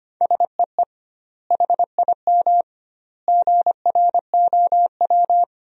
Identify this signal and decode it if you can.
Morse code, heard as a keyed tone, about 25 wpm, 700 Hz. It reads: SEE HIM GROW
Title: 2x Word-Spacing - Sentences from Top 300 Words - 25wpm